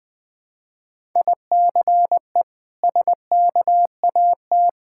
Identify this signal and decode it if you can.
Morse code, heard as a keyed tone, about 21 wpm, 700 Hz. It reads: ICE SKAT
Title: Difficult Words with Hints - 20wpm